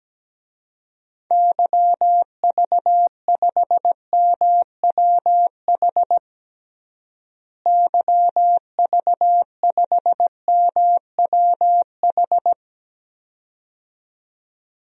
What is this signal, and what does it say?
Morse code, heard as a keyed tone, about 17 wpm, 700 Hz. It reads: YV5MWH YV5MWH
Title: DXCC-100 Call Signs - 6 Characters - Repeated 2x - 17wpm